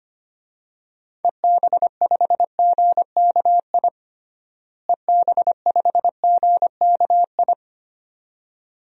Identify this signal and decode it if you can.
Morse code, heard as a keyed tone, about 25 wpm, 700 Hz. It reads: EB5GKI EB5GKI
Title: DXCC-100 Call Signs - 6 Characters - Repeated 2x - 25wpm